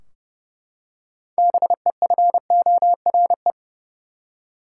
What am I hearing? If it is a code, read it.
Morse code, heard as a keyed tone, about 31 wpm, 700 Hz. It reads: BEFORE
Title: Six Letter Words - 30wpm